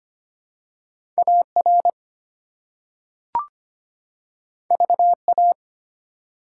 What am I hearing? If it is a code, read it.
Morse code, heard as a keyed tone, about 25 wpm, 700 Hz. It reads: AR VA
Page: morse-code-ninja-practice-25wpm